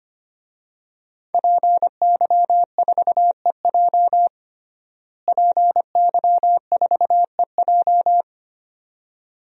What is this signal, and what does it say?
Morse code, heard as a keyed tone, about 25 wpm, 700 Hz. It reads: PY4EJ PY4EJ
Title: DXCC-100 Call Signs - 5 Characters - Repeated 2x - 25wpm